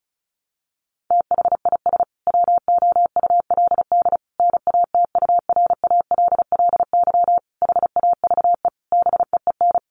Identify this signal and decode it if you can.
Morse code, heard as a keyed tone, about 34 wpm, 700 Hz. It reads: THIS WOULD NATURALLY HAVE BEEN
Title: Sentences from Top 1300 Words 35wpm